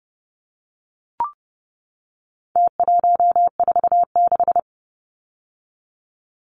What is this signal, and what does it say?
Morse code, heard as a keyed tone, about 30 wpm, 700 Hz. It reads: T146
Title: Numbers - Contest Style - 4-Digit 30wpm